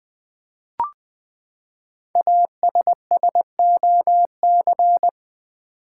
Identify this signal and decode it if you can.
Morse code, heard as a keyed tone, about 20 wpm, 700 Hz. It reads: ASSOC
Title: Build A Word - 5 Letters At A Time - 10-Letter Words - 20wpm